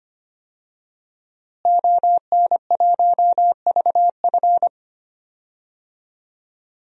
Morse code, heard as a keyed tone, about 25 wpm, 700 Hz. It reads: ON1VF
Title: DXCC-100 Call Signs - 5 Characters - Repeated 3x - 25wpm